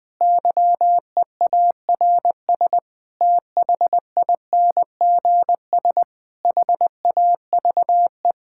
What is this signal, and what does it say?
Morse code, heard as a keyed tone, about 20 wpm, 700 Hz. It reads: YEARS THINGS HAVE
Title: Sentences with 5-Word Phrases - Spoken Beginning - 20wpm